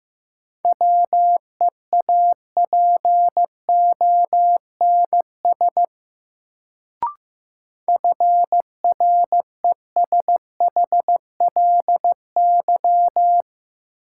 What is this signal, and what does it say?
Morse code, heard as a keyed tone, about 15 wpm, 700 Hz. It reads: WEAPONS FRESHLY
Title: Sets of 5 Words Encore - New Word Drill - 15wpm